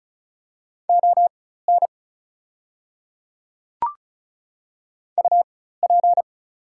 Morse code, heard as a keyed tone, about 36 wpm, spaced as for 16 wpm, 690 Hz. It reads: ON UP
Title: 4x Character Spacing - Top 100 Words - 35wpm